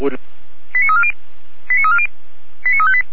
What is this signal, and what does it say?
Five-tone selective call (ZVEI), heard as quick runs of five tones, tones of about 70 ms, with speech. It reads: 89340 89340 89358